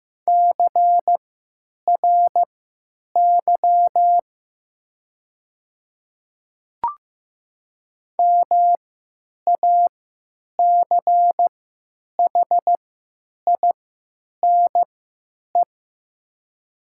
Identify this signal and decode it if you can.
Morse code, heard as a keyed tone, about 15 wpm, spaced as for 9 wpm, 700 Hz. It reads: CRY MACHINE
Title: 3x Character Spacing - Top 500 Words - 15wpm